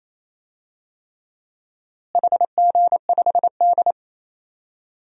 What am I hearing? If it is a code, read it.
Morse code, heard as a keyed tone, about 28 wpm, 700 Hz. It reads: HG5D
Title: DXCC-100 Call Signs - 4 Characters - Repeated 3x - 28wpm